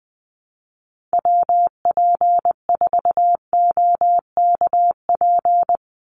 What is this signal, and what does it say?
Morse code, heard as a keyed tone, about 20 wpm, 700 Hz. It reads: WP4OKP
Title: USA Call Signs 20wpm